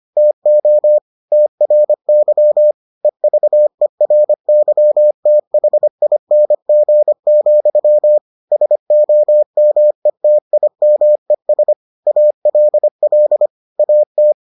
Morse code, heard as a keyed tone, about 25 wpm, 600 Hz. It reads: TO TRY EVERYTHING, SOMETIMES ALL AT